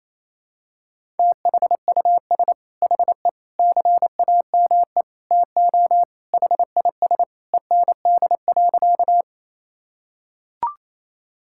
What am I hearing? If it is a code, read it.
Morse code, heard as a keyed tone, about 28 wpm, 700 Hz. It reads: THUS HE CAME TO HIS END.
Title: Sentences from Top 1000 Words - Review Part 5 - 28wpm